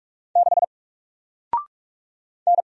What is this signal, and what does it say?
Morse code, heard as a keyed tone, about 44 wpm, 700 Hz. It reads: 6 N
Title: Single Character (Letter:Number:Punctuation:Prosign) 45wpm